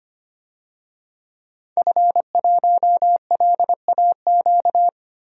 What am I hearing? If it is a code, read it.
Morse code, heard as a keyed tone, about 25 wpm, 700 Hz. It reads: F1LAQ
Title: DXCC-100 Call Signs - 5 Characters - Repeated 2x - 25wpm